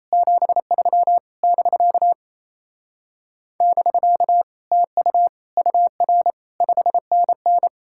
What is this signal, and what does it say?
Morse code, heard as a keyed tone, about 38 wpm, 700 Hz. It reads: 73 <BK> <BK> TU UR 5NN